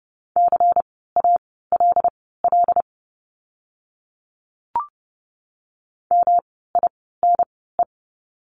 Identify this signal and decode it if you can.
Morse code, heard as a keyed tone, about 30 wpm, spaced as for 17 wpm, 700 Hz. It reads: CALL MINE
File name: course / 30:10wpm / 108 Lesson 18f - Letter m - Words 30wpm